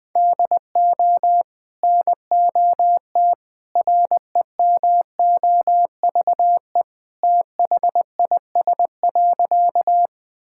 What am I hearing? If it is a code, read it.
Morse code, heard as a keyed tone, about 21 wpm, 700 Hz. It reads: DO NOT REMOVE THIS.
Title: Sentences from Top 1000 Words - Review Part 5 - 20wpm